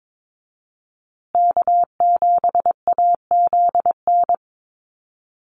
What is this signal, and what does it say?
Morse code, heard as a keyed tone, about 22 wpm, 700 Hz. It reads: K7AZN
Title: USA Call Signs Repeated 2x 22wpm